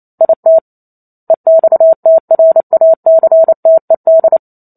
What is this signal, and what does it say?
Morse code, heard as a keyed tone, about 28 wpm, 650 Hz. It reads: IT EXTRACTED